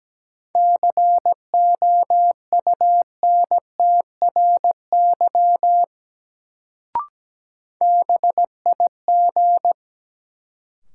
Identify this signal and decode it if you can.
Morse code, heard as a keyed tone, about 17 wpm, 700 Hz. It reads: COUNTRY BIG
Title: Sets of 2 Words - New Word Drill - 17wpm